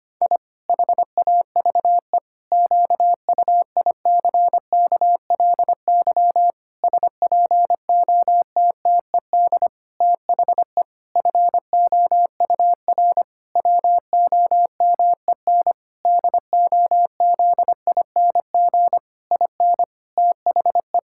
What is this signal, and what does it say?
Morse code, heard as a keyed tone, about 25 wpm, 700 Hz. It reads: I HAVE QUICKLY SPOTTED THE FOUR WOMEN DOZING IN THE
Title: Pangram Challenge v2 - 25wpm